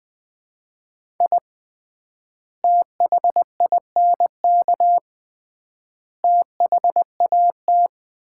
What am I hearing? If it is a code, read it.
Morse code, heard as a keyed tone, about 20 wpm, 700 Hz. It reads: I THINK THAT
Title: 3x Word-Spacing - Sets of 3 Words - 20wpm